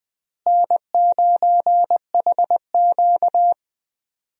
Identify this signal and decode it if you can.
Morse code, heard as a keyed tone, about 20 wpm, 700 Hz. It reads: N9HQ
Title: USA Call Signs - 4 Characters - Repeated 2x - 20wpm